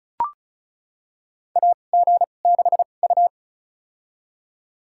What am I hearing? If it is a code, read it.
Morse code, heard as a keyed tone, about 36 wpm, spaced as for 25 wpm, 700 Hz. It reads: AG6U
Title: USA Call Signs - 2x Character Spacing - 35wpm